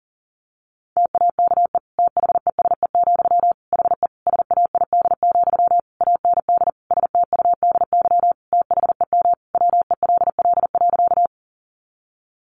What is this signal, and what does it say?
Morse code, heard as a keyed tone, about 40 wpm, 700 Hz. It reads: TAKE THESE, HE SAID, AND STUDY THEM WELL.
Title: Sentences from Top 300 Words 40wpm